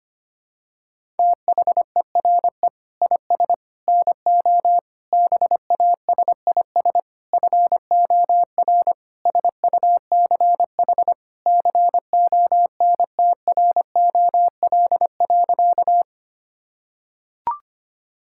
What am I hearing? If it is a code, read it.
Morse code, heard as a keyed tone, about 25 wpm, 700 Hz. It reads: THERE IS NO BASIS FOR SUCH CONTROL.
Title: Sentences from Top 1000 Words - Review Part 5 - 25wpm